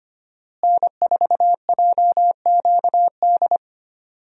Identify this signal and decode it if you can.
Morse code, heard as a keyed tone, about 25 wpm, 700 Hz. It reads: N4JQD
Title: USA Call Signs - 5 Characters - Repeated 3x - 25wpm